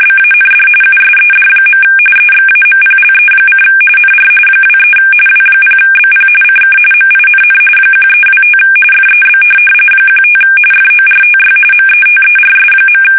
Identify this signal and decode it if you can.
RTTY, about 45.45 baud. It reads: BAND USED. EXAMPLE: NR 23 W0BP CK MINN 1325 FEB 15 FORTY METERS.
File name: fsk850 / teletype-RTTY